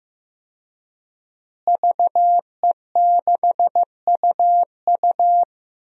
Morse code, heard as a keyed tone, about 15 wpm, 700 Hz. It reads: VE6UU
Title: DXCC-100 Call Signs - 5 Characters - Repeated 2x - 15wpm